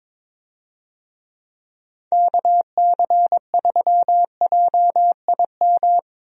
Morse code, heard as a keyed tone, about 22 wpm, 700 Hz. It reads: KC3JIM
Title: USA Call Signs - 6 Characters - Repeated 3x - 22wpm